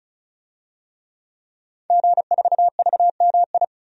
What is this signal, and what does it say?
Morse code, heard as a keyed tone, about 34 wpm, 700 Hz. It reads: G4VMI
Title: DXCC-100 Call Signs - 5 Characters - 35wpm